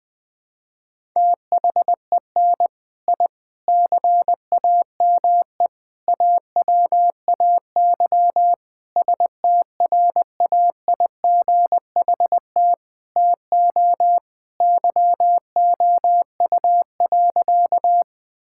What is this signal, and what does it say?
Morse code, heard as a keyed tone, about 20 wpm, 700 Hz. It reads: THEN I CAME AWAY STRAIGHT TO YOU.
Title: Sentences from Top 1000 Words - Review Part 6 - 20wpm